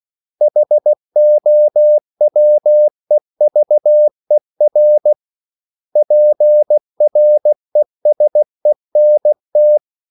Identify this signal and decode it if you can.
Morse code, heard as a keyed tone, about 16 wpm, 600 Hz. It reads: HOWEVER PRESENT
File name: CW / Practice / 500MostCommonEnglishWords16